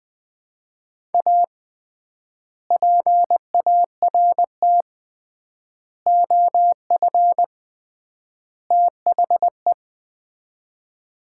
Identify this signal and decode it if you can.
Morse code, heard as a keyed tone, about 20 wpm, 700 Hz. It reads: A PART OF THE
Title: 3x Word-Spacing - Sets of 4 Words - 20wpm